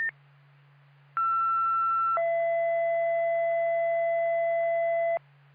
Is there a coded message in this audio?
Yes: two-tone page from 1.2 to 5.2 s, heard as a one-second tone, then a three-second tone, about 1340 then 670 Hz, with noise.